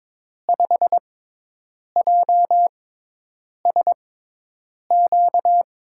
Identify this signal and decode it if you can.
Morse code, heard as a keyed tone, about 21 wpm, spaced as for 8 wpm, 700 Hz. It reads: 5JSQ
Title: USA Call Signs - 6 Characters - 6x Character Spacing - 22wpm